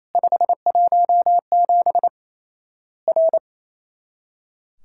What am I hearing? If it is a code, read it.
Morse code, heard as a keyed tone, about 28 wpm, 700 Hz. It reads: 517 R
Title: Worked All Europe DX Contest - QTC Exchange - 28wpm